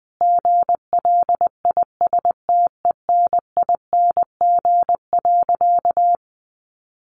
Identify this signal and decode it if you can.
Morse code, heard as a keyed tone, about 20 wpm, 700 Hz. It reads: GLISTENING.
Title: A Sound Of Thunder 20wpm